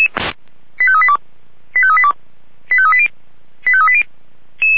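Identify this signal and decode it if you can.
Five-tone selective call (ZVEI), heard as quick runs of five tones, tones of about 70 ms. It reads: 85382 85382 85380 85380
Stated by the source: Received on an SDR; ZVEI five-tone call 89155.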